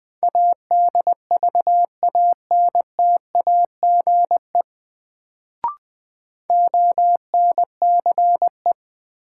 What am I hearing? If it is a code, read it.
Morse code, heard as a keyed tone, about 21 wpm, 700 Hz. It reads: ADVANTAGE ONCE
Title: Sets of 4 Words - New Word Drill - 20wpm